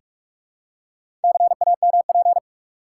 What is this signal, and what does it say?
Morse code, heard as a keyed tone, about 46 wpm, 700 Hz. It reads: CAMP
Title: Four Letter Words - 45wpm